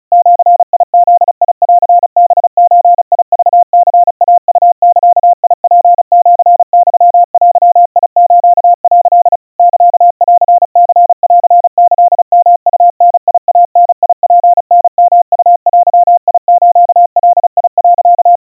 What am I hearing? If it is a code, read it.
Morse code (Japanese Wabun code), heard as a keyed tone, about 36 wpm, 700 Hz. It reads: ジブンハゾクニイウザツシユデス」 サンニンキヨウダイダツタヨウデスガ、
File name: kawa / Morse sample / kamyu 130 cpm 700 Hz Crispy